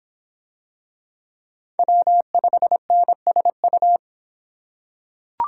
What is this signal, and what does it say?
Morse code, heard as a keyed tone, about 25 wpm, 700 Hz. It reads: W5NSU